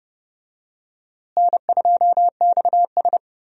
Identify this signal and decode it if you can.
Morse code, heard as a keyed tone, about 30 wpm, 700 Hz. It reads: N2XS
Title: USA Call Signs - 4 Characters - Repeated 2x - 30wpm